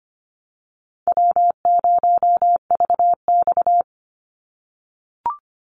Morse code, heard as a keyed tone, about 25 wpm, 700 Hz. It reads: W0VX
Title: Easy Call Signs - 25wpm